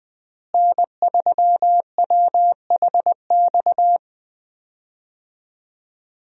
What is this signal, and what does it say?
Morse code, heard as a keyed tone, about 20 wpm, 700 Hz. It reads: N3WHX
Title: USA Call Signs - 5 Characters - Repeated 3x - 20wpm